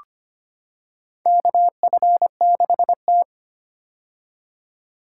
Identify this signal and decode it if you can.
Morse code, heard as a keyed tone, about 25 wpm, 700 Hz. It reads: KF6T
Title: K1USN SST QSO Practice - 25wpm